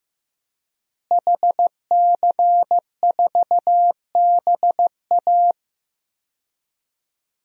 Morse code, heard as a keyed tone, about 15 wpm, 700 Hz. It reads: HC4BA
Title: DXCC-100 Call Signs - 5 Characters - Repeated 3x - 15wpm